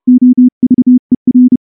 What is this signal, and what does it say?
Morse code, heard as a keyed tone, about 30 wpm, 250 Hz. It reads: OVER
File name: morse-training-30WPM